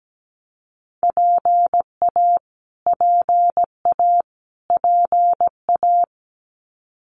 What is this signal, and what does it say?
Morse code, heard as a keyed tone, about 17 wpm, 700 Hz. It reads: PA PA PA
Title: US States Repeated 3x 17wpm